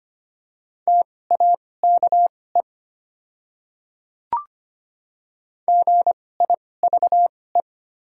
Morse code, heard as a keyed tone, about 25 wpm, spaced as for 18 wpm, 700 Hz. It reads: TAKE GIVE
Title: 2x Character Spacing - Top 300 Words - 25wpm